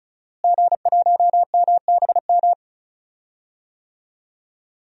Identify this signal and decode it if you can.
Morse code, heard as a keyed tone, about 34 wpm, 700 Hz. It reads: G1MBM
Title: DXCC-100 Call Signs - 5 Characters - Repeated 2x - 35wpm